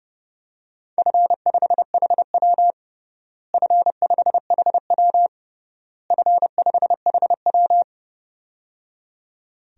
Morse code, heard as a keyed tone, about 30 wpm, 700 Hz. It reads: F5HW F5HW F5HW
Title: DXCC-100 Call Signs - 4 Characters - Repeated 3x - 30wpm